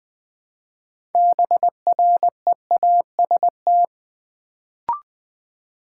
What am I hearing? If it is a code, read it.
Morse code, heard as a keyed tone, about 20 wpm, 700 Hz. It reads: BREAST
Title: Build A Word - 3 Letters At A Time - 6-Letter Words - 20wpm